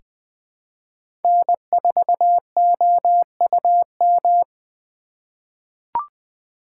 Morse code, heard as a keyed tone, about 20 wpm, 700 Hz. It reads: N4OUM
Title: USA Call Signs - 5 Characters - 4x Character Spacing - 20wpm